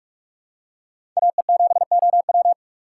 Morse code, heard as a keyed tone, about 43 wpm, 700 Hz. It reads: AE7OW
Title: USA Call Signs - 5 Characters - 3x Character Spacing - 45wpm